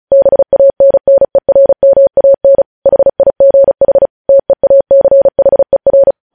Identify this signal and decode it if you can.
Morse code, heard as a keyed tone, about 36 wpm, 550 Hz. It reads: BANNERMAN HIGH TEACHER
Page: news headlines in morse code at 35 wpm